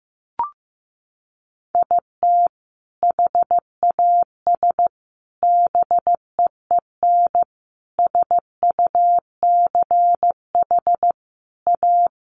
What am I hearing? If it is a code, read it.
Morse code, heard as a keyed tone, about 15 wpm, 700 Hz. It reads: IT HAS BEEN SUCH A
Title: Sentences from Top 1100 Words 15wpm